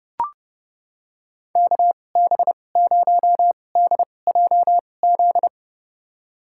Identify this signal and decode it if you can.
Morse code, heard as a keyed tone, about 30 wpm, spaced as for 22 wpm, 700 Hz. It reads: KB0DJZ
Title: USA Call Signs - 2x Character Spacing - 30wpm